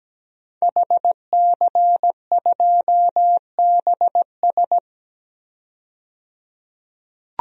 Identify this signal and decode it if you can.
Morse code, heard as a keyed tone, about 17 wpm, 700 Hz. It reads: HC2BS
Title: DXCC-100 Call Signs - 5 Characters - 17wpm